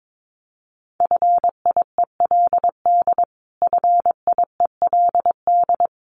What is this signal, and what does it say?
Morse code, heard as a keyed tone, about 22 wpm, 700 Hz. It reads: FIELD FIELD
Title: Top 500 Words Repeated 22wpm